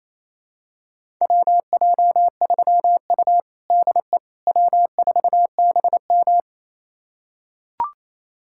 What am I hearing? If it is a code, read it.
Morse code, heard as a keyed tone, about 28 wpm, 700 Hz. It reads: WJ3U DE W4BM
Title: USA Callsign DE Callsign - 4 Characters - 28wpm